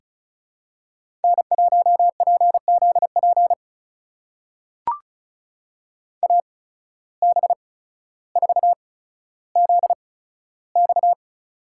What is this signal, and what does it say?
Morse code, heard as a keyed tone, about 36 wpm, spaced as for 15 wpm, 700 Hz. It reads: N1PZP AB4ZX
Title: USA Call Signs - 5 Characters - 8x Character Spacing - 35wpm